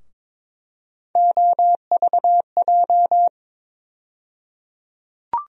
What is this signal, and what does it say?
Morse code, heard as a keyed tone, about 22 wpm, 700 Hz. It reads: OVJ
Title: USA Call Sign Postfixes - 3 Letters 22wpm